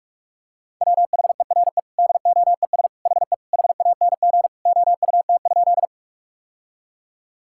Morse code, heard as a keyed tone, about 46 wpm, 700 Hz. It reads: WHERE DOES HE HANG OUT?
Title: Sentences from Top 1600 Words 45wpm